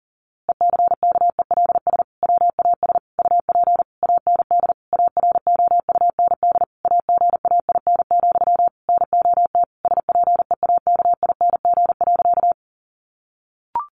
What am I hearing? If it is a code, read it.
Morse code, heard as a keyed tone, about 40 wpm, 700 Hz. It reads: ECKELS WAS UP AND AROUND AGAIN, NOT SPEAKING.
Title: A Sound Of Thunder 40wpm